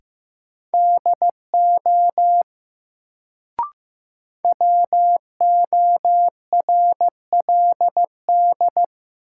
Morse code, heard as a keyed tone, about 15 wpm, 700 Hz. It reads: DO WORLD
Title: Sets of 2 Words - New Word Drill - 15wpm